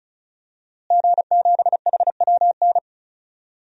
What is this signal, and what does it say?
Morse code, heard as a keyed tone, about 36 wpm, 700 Hz. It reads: G7HWN
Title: DXCC-100 Call Signs - 5 Characters - Repeated 2x - 35wpm